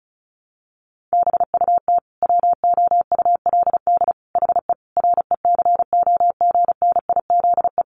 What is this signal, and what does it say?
Morse code, heard as a keyed tone, about 34 wpm, 700 Hz. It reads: BUT WOULD HE RECOGNIZE
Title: Sentences from Top 1400 Words 35wpm